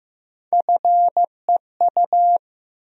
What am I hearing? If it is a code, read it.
Morse code, heard as a keyed tone, about 15 wpm, 700 Hz. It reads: FEU
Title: USA Call Sign Postfixes - 3 Letters 15wpm